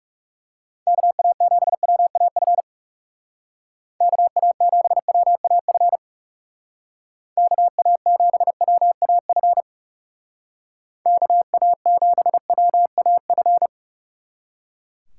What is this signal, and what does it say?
Morse code, heard as a keyed tone, about 36 wpm, 700 Hz. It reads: KA7WAF KA7WAF KA7WAF KA7WAF
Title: Speed Racing - USA 2x3 Call Signs 30wpm